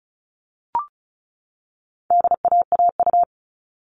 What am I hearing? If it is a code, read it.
Morse code, heard as a keyed tone, about 34 wpm, 700 Hz. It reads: DAAU
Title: All Cut Numbers - 4-Digit 35wpm